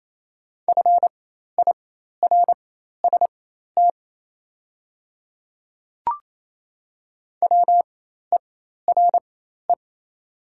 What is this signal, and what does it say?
Morse code, heard as a keyed tone, about 28 wpm, spaced as for 13 wpm, 700 Hz. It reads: FIRST WERE
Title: 4x Character Spacing - Top 500 Words - 28wpm